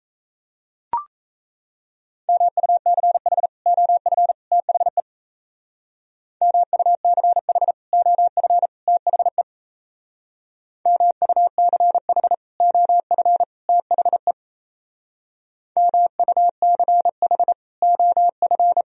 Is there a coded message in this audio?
Yes: Morse code from 2.3 to 18.8 s, heard as a keyed tone, about 37 wpm, 700 Hz. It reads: MUCH OF THE MUCH OF THE MUCH OF THE MUCH OF